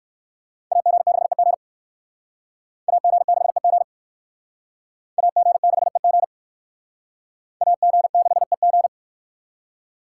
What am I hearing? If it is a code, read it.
Morse code, heard as a keyed tone, about 56 wpm, 700 Hz. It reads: AG6EG AG6EG AG6EG AG6EG
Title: Speed Racing - USA Call Signs 45wpm